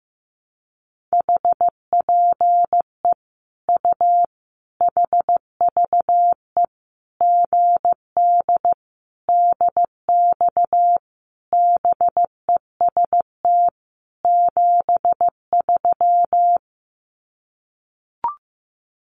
Morse code, heard as a keyed tone, about 15 wpm, 700 Hz. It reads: HPE U HVE GD DX BEST 73
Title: Common QSO Phrases 15wpm